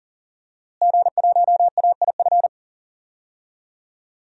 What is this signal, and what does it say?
Morse code, heard as a keyed tone, about 40 wpm, 700 Hz. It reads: G1AIF
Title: DXCC-100 Call Signs - 5 Characters - Repeated 3x - 40wpm